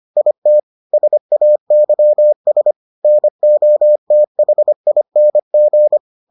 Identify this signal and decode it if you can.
Morse code, heard as a keyed tone, about 25 wpm, 600 Hz. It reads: IT SAYS NOTHING